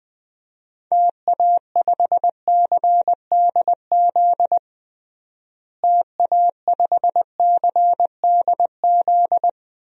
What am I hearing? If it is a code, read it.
Morse code, heard as a keyed tone, about 20 wpm, 700 Hz. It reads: TA5CDZ TA5CDZ
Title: DXCC-100 Call Signs - 6 Characters - Repeated 2x - 20wpm